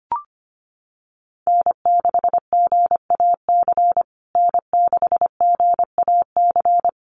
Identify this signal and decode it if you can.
Morse code, heard as a keyed tone, about 25 wpm, 700 Hz. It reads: N6GAC N6GAC
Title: USA Call Signs Repeated 3x 25wpm